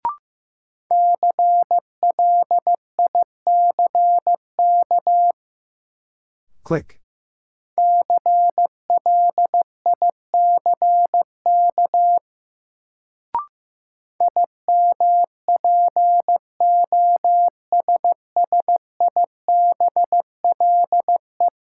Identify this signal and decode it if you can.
Morse code, heard as a keyed tone, about 15 wpm, 700 Hz. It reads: CLICK CLICK IMPOSSIBLE
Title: Sentences from Top 1100 Words - New Word Drill - 15wpm